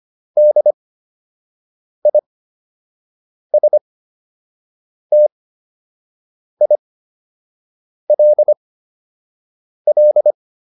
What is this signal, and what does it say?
Morse code, heard as a keyed tone, about 25 wpm, spaced as for 6 wpm, 600 Hz. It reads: DISTILL